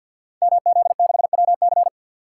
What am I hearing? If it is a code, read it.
Morse code, heard as a keyed tone, about 52 wpm, 700 Hz. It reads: MZ6WC
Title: Code Groups - 5 Letter:Numbers - 50wpm